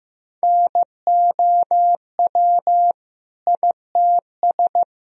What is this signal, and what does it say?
Morse code, heard as a keyed tone, about 15 wpm, 700 Hz. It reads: NOW ITS
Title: Sentences from Top 1000 Words - Review Part 1 - 15wpm